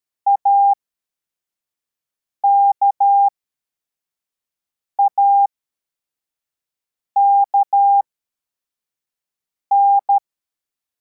Morse code, heard as a keyed tone, about 13 wpm, spaced as for 4 wpm, 800 Hz. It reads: AKAKN
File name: formation / cw / Text11